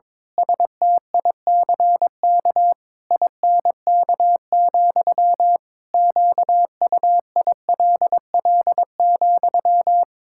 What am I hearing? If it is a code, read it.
Morse code, heard as a keyed tone, about 22 wpm, 700 Hz. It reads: STICK INK, QUILL,
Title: Pangram Challenge v2 - 22wpm